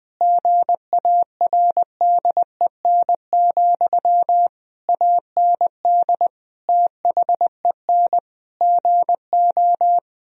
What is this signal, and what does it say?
Morse code, heard as a keyed tone, about 21 wpm, 700 Hz. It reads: GARDEN, AND THEN GO
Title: Sentences from Top 1000 Words - Review Part 4 - 20wpm